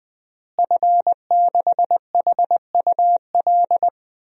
Morse code, heard as a keyed tone, about 21 wpm, 700 Hz. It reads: F6HUL
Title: DXCC-100 Call Signs - 5 Characters - Repeated 2x - 20wpm